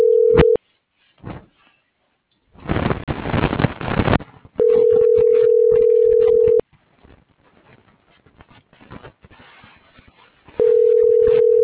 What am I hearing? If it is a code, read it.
Ringback tone (North America), 440 + 480 Hz.